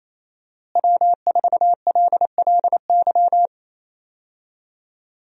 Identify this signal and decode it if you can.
Morse code, heard as a keyed tone, about 28 wpm, 700 Hz. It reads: W4LLY
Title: USA Call Signs - 5 Characters - Repeated 2x - 28wpm